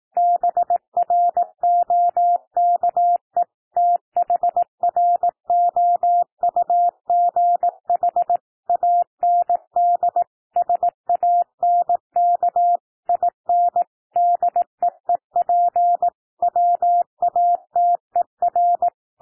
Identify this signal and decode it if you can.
Morse code, heard as a keyed tone, about 18 wpm, 700 Hz. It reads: BROKE THROUGH AND SANK IN DEEP WATER
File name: TEXT PASSAGE 18 wpm